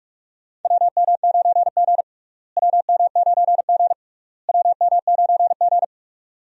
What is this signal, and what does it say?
Morse code, heard as a keyed tone, about 47 wpm, 700 Hz. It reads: WM9G WM9G WM9G
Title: USA Call Signs - 4 Characters - Repeated 3x - 45wpm